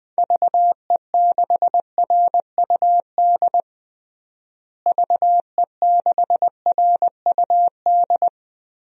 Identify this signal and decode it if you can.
Morse code, heard as a keyed tone, about 19 wpm, 700 Hz. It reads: VE6RUD VE6RUD
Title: DXCC-100 Call Signs - 6 Characters - Repeated 3x - 20wpm